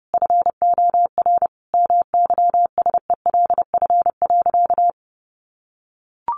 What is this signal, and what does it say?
Morse code, heard as a keyed tone, about 30 wpm, 700 Hz. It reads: FOR MYSELF.
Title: Sentences from Top 1300 Words 30wpm